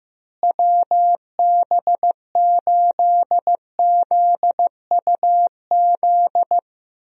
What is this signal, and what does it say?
Morse code, heard as a keyed tone, about 15 wpm, 700 Hz. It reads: WB8ZUZ
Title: USA Call Signs - 6 Characters - Repeated 2x - 15wpm